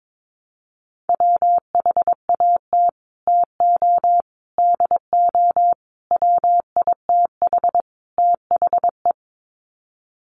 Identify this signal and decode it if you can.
Morse code, heard as a keyed tone, about 22 wpm, 700 Hz. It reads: WHAT TO DO WITH THE
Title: Sets of 5 Words Encore2 22wpm